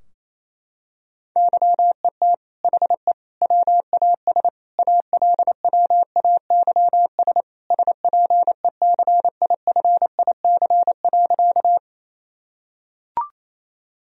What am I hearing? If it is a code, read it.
Morse code, heard as a keyed tone, about 28 wpm, 700 Hz. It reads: YET HE WAS ALWAYS SPECIFIC.
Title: Sentences from Top 1000 Words - Review Part 3 - 28wpm